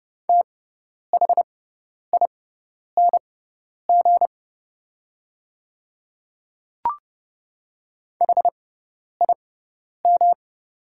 Morse code, heard as a keyed tone, about 30 wpm, spaced as for 10 wpm, 700 Hz. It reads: THING HIM
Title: 6x Character Spacing - Top 200 Words - 30wpm